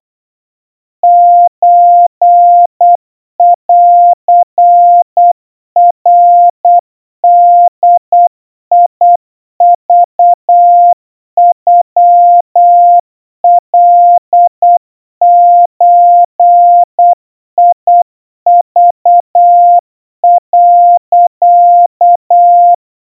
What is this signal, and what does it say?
Morse code (Japanese Wabun code), heard as a keyed tone, about 8 wpm, 700 Hz. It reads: ソンナボクノカゾク、
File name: kawa / Morse sample / kamyu 30 cpm 700 Hz Crispy